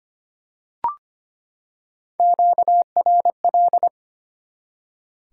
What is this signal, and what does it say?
Morse code, heard as a keyed tone, about 25 wpm, 700 Hz. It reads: QRL
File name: practice / 25wpm / Top 20 QSO Elements - 25wpm